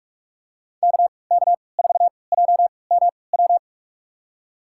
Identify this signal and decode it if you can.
Morse code, heard as a keyed tone, about 46 wpm, spaced as for 25 wpm, 700 Hz. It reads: KK4JMW
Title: USA Call Signs - 6 Characters - 3x Character Spacing - 45wpm